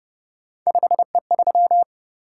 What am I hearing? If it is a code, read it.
Morse code, heard as a keyed tone, about 31 wpm, 700 Hz. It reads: 5E3
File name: Speed-Racing - Three Letter-Number Combinations - 25wpm